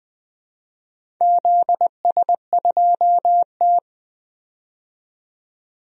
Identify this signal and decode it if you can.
Morse code, heard as a keyed tone, about 21 wpm, 700 Hz. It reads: ZS2T
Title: DXCC-100 Call Signs - 4 Characters - 20wpm